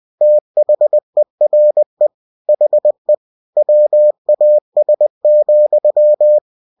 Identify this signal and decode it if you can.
Morse code, heard as a keyed tone, about 20 wpm, 600 Hz. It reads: THERE HE WAS,
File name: cw-books / winnie-the-pooh / winnie-20wpm-page-0094